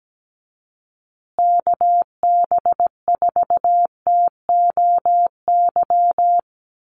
Morse code, heard as a keyed tone, about 17 wpm, 700 Hz. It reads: KB4TOY
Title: USA Call Signs Repeated 2x 17wpm